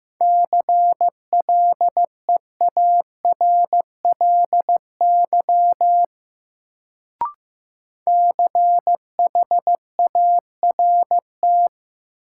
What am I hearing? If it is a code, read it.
Morse code, heard as a keyed tone, about 15 wpm, 700 Hz. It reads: CLEARLY CHART
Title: Sentences from Top 1300 Words - New Word Drill - 15wpm